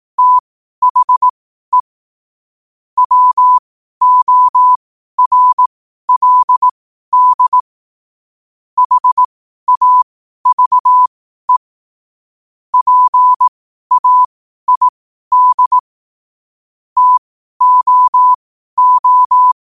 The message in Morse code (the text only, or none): THE WORLD HAVE PAID TOO